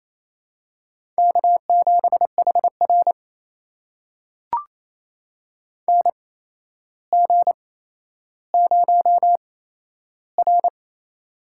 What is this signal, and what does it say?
Morse code, heard as a keyed tone, about 28 wpm, spaced as for 12 wpm, 700 Hz. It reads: K7HR NG0R
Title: USA Call Signs - 8x Character Spacing - 28wpm